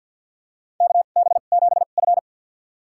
Morse code, heard as a keyed tone, about 51 wpm, spaced as for 36 wpm, 700 Hz. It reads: KB7F
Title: USA Call Signs - 4 Characters - 2x Character Spacing - 50wpm